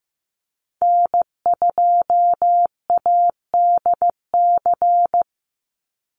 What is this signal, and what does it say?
Morse code, heard as a keyed tone, about 15 wpm, 700 Hz. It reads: N2ADC
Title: Call Signs Repeated 3x - 15wpm